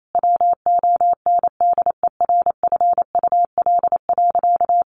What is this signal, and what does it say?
Morse code, heard as a keyed tone, about 28 wpm, 700 Hz. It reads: WONDERFUL.
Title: Sentences from Top 1100 Words 28wpm